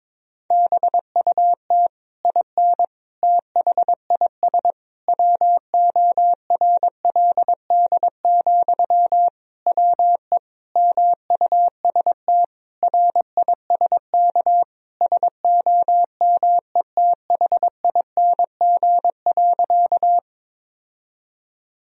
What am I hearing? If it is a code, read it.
Morse code, heard as a keyed tone, about 22 wpm, 700 Hz. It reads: BUT IN THIS WORLD, WE MUST RISK SOMETHING.
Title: Sentences from Top 1000 Words - Review Part 4 - 22wpm